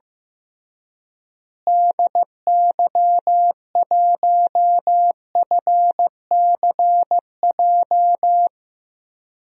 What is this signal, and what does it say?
Morse code, heard as a keyed tone, about 15 wpm, 700 Hz. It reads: DY1FCJ
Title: DXCC-100 Call Signs - 6 Characters - Repeated 3x - 15wpm